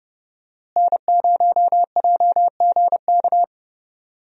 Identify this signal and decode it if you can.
Morse code, heard as a keyed tone, about 30 wpm, 700 Hz. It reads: N0JGK
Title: USA Call Signs - 5 Characters - Repeated 2x - 30wpm